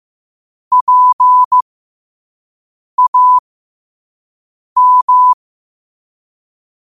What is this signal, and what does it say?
Morse code, heard as a keyed tone, about 15 wpm, spaced as for 5 wpm, 1000 Hz. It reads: PAM
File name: qotdfiles / 5wpm